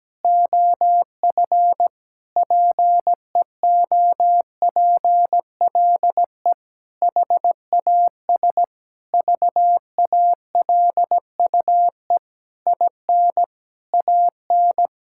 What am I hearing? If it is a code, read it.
Morse code, heard as a keyed tone, about 17 wpm, 700 Hz. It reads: OF PEOPLE HAS VALUE IN AN
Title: Sentences from Top 1000 Words - Review Part 6 - 17wpm